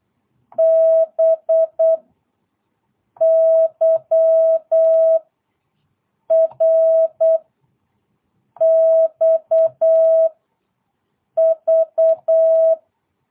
Morse code, heard as a keyed tone, about 8 wpm, 660 Hz. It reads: B Y R X V
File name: jh8cbh / 2022cw